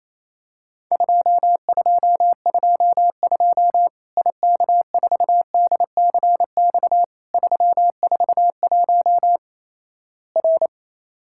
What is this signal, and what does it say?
Morse code, heard as a keyed tone, about 27 wpm, 700 Hz. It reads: 2222 IK4DCX 341 R
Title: Worked All Europe DX Contest - QTC Exchange - 28wpm